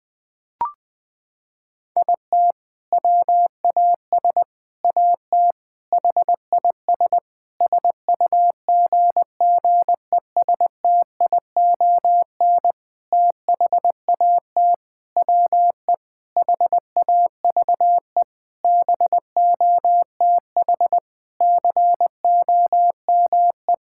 Morse code, heard as a keyed tone, about 20 wpm, 700 Hz. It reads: IT WAS AT HIS SUGGESTION THAT WE HAVE BOTH COME
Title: Sentences from Top 1600 Words 20wpm